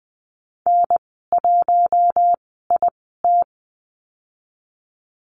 Morse code, heard as a keyed tone, about 20 wpm, spaced as for 14 wpm, 700 Hz. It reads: N1IT
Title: 062 Lesson 11g - Letter h - Call Signs 20wpm